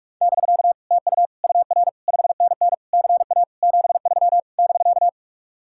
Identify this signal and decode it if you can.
Morse code, heard as a keyed tone, about 49 wpm, 700 Hz. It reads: <BK> TU UR 5NN CA 73 <BK>